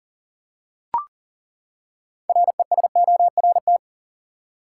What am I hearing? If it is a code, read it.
Morse code, heard as a keyed tone, about 40 wpm, 700 Hz. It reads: RESORT
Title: Six Letter Words - 40wpm